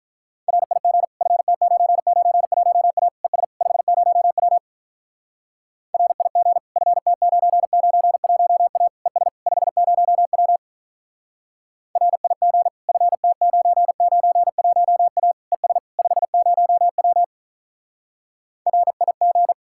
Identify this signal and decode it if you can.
Morse code, heard as a keyed tone, about 46 wpm, 700 Hz. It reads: RIG FT991A ES 50W RIG FT991A ES 50W RIG FT991A ES 50W RIG